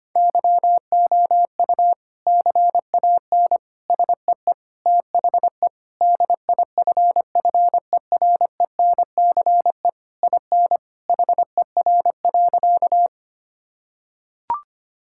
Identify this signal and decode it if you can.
Morse code, heard as a keyed tone, about 25 wpm, 700 Hz. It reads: YOU CAN SEE THE DIFFERENCE IN HER.
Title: Sentences from Top 1000 Words - Review Part 5 - 25wpm